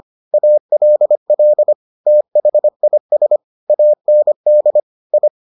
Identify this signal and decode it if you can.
Morse code, heard as a keyed tone, about 25 wpm, 600 Hz. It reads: ALL THIS AND I